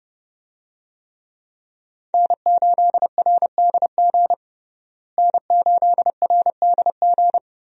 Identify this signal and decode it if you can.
Morse code, heard as a keyed tone, about 32 wpm, 700 Hz. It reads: N8RDG N8RDG
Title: USA Call Signs - 5 Characters - Repeated 2x - 30wpm